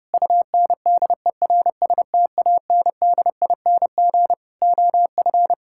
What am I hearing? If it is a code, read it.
Morse code, heard as a keyed tone, about 30 wpm, 700 Hz. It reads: UNDERSTANDING OF
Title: Sentences from Top 1000 Words - Review Part 6 - 30wpm